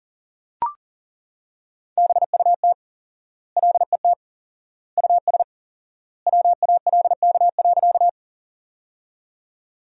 Morse code, heard as a keyed tone, about 40 wpm, 700 Hz. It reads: BUT LET US WALK.
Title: 4x Word-Spacing - Sentences from Top 300 Words - 40wpm